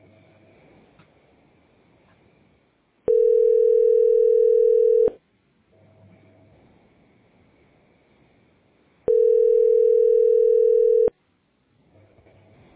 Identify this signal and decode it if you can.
Ringback tone (North America), 440 + 480 Hz.